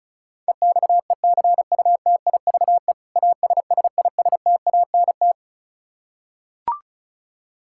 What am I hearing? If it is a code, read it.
Morse code, heard as a keyed tone, about 34 wpm, 700 Hz. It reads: EXECUTIVE ASSISTANT
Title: Practice QSO - Exchange 3 - More Info - 35wpm